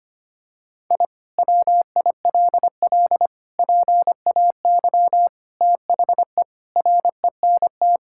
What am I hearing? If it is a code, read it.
Morse code, heard as a keyed tone, about 25 wpm, 700 Hz. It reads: I WILL PAY THE RENT
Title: Sentences from Top 1000 Words - Review Part 5 - 25wpm